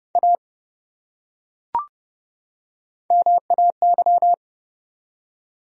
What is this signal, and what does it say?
Morse code, heard as a keyed tone, about 31 wpm, 700 Hz. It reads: A MAY
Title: Top 100 Words 30wpm